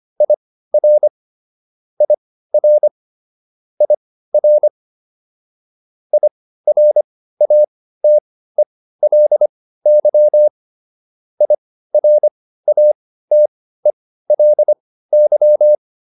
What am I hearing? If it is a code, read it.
Morse code, heard as a keyed tone, about 25 wpm, spaced as for 15 wpm, 600 Hz. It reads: IR IR IR IRATELY IRATELY